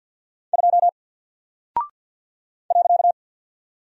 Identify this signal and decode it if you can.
Morse code, heard as a keyed tone, about 48 wpm, 700 Hz. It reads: 2 .
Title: Single Character (Letter:Number:Punctuation:Prosign) 50wpm